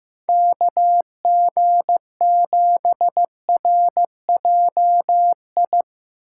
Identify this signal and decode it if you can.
Morse code, heard as a keyed tone, about 16 wpm, 700 Hz. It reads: KG7RJI
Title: Speed Racing - USA 2x3 Call Signs 15wpm